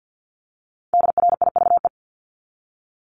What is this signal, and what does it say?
Morse code, heard as a keyed tone, about 50 wpm, 700 Hz. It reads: DRIVE
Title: Multi-letter Phonemes 50wpm